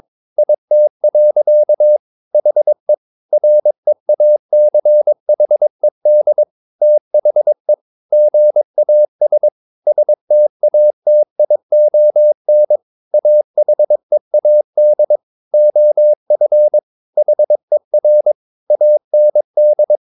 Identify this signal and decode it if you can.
Morse code, heard as a keyed tone, about 22 wpm, 600 Hz. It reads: IT. HE REACHED THE GAS STATION AHEAD OF HER AND
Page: Romanen Never af Ken Follet på engelsk.